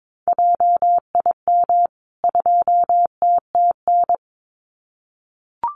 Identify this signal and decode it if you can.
Morse code, heard as a keyed tone, about 22 wpm, 700 Hz. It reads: JIM 2TTN
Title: CWT Exchange 22wpm